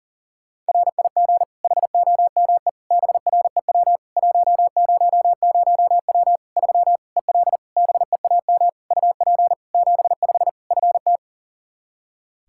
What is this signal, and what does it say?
Morse code, heard as a keyed tone, about 40 wpm, 700 Hz. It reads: RIG HOME BREW 100W 3 EL BEAM UP 75 FT
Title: K1USN SST QSO Practice 40wpm